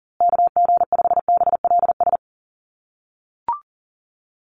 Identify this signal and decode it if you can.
Morse code, heard as a keyed tone, about 40 wpm, 700 Hz. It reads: KG5BLS
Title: USA Call Signs Repeated 2x 40wpm